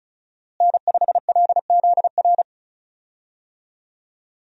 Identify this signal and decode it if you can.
Morse code, heard as a keyed tone, about 36 wpm, 700 Hz. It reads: N5LZR
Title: USA Call Signs - 5 Characters - Repeated 3x - 35wpm